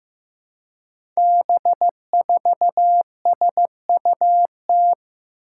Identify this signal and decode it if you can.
Morse code, heard as a keyed tone, about 15 wpm, 700 Hz. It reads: B4SUT
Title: Five Challenging Letter-Number Combinations 15wpm